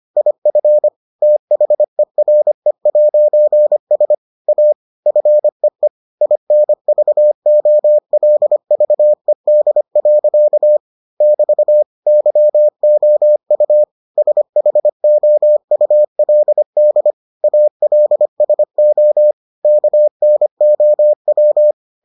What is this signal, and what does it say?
Morse code, heard as a keyed tone, about 25 wpm, 600 Hz. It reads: IF THERE'S A FEE INVOLVED. <BT> YOU SHOULD ALSO KNOW